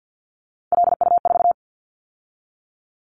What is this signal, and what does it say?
Morse code, heard as a keyed tone, about 50 wpm, 700 Hz. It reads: LU4
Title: Callsign Prefixes for Top 100 DXCC - 50wpm